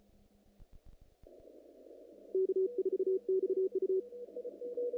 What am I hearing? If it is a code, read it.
Morse code, heard as a keyed tone, about 34 wpm, 380 Hz, with radio noise. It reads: K4XU
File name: contests / naqp-cw-k5tr-op-k5pi- -1900